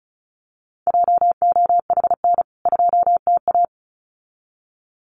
Morse code, heard as a keyed tone, about 34 wpm, 700 Hz. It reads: JOHN 2TA
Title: CWT Exchange 35wpm